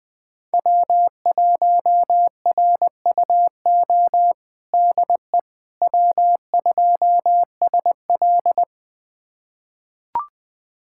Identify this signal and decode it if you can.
Morse code, heard as a keyed tone, about 21 wpm, 700 Hz. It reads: W1RUO DE W2SL
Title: USA Callsign DE Callsign 20wpm